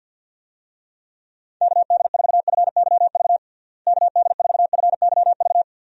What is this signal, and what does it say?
Morse code, heard as a keyed tone, about 50 wpm, 700 Hz. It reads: KD4FYV KD4FYV
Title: USA Call Signs - 6 Characters - Repeated 3x - 50wpm